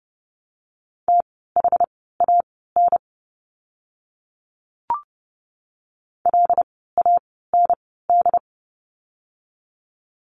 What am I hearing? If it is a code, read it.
Morse code, heard as a keyed tone, about 30 wpm, spaced as for 17 wpm, 700 Hz. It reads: THAN LAND
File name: Lesson 17f - Letter u - Words 30wpm